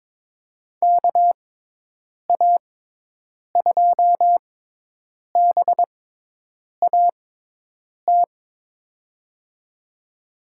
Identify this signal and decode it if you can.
Morse code, heard as a keyed tone, about 22 wpm, spaced as for 8 wpm, 700 Hz. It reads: KA2BAT
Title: USA Call Signs - 6 Characters - 6x Character Spacing - 22wpm